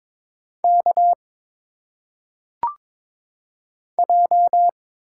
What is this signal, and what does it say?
Morse code, heard as a keyed tone, about 22 wpm, 700 Hz. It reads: K J
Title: Single Letters - Phonetic Pronunciation 22wpm